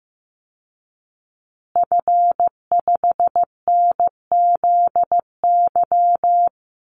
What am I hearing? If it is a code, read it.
Morse code, heard as a keyed tone, about 15 wpm, 700 Hz. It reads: F5NZY
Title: Call Signs Repeated 2x - 15wpm